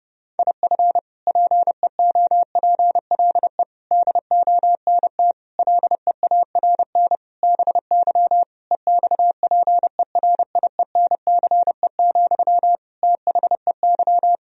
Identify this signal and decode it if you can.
Morse code, heard as a keyed tone, about 30 wpm, 700 Hz. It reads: IF PEOPLE DONT LEARN BY EXPERIENCE, THEY
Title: Sentences from Top 1000 Words - Review Part 3 - 30wpm